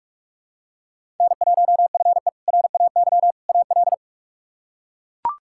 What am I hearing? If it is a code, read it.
Morse code, heard as a keyed tone, about 42 wpm, 700 Hz. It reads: N1FE RAY AL
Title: K1USN SST QSO Practice - 45wpm